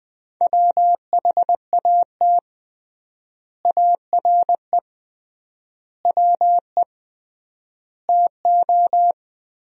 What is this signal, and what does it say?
Morse code, heard as a keyed tone, about 20 wpm, 700 Hz. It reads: WHAT ARE WE TO
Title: 3x Word-Spacing - Sentences from Top 100 Words - 20wpm